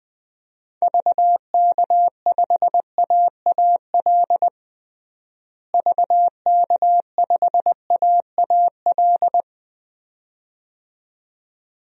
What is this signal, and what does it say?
Morse code, heard as a keyed tone, about 20 wpm, 700 Hz. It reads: VK5AAL VK5AAL
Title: DXCC-100 Call Signs - 6 Characters - Repeated 2x - 20wpm